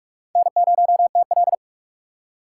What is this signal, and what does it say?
Morse code, heard as a keyed tone, about 45 wpm, 700 Hz. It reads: N0TL
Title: USA Call Signs - 4x Character Spacing - 45wpm